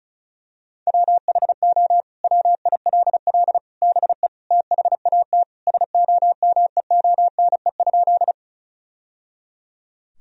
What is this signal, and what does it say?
Morse code, heard as a keyed tone, about 36 wpm, 700 Hz. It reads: WHO WILL BE THAT SOMEONE?
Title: Sentences from Top 1000 Words - Review Part 4 - 35wpm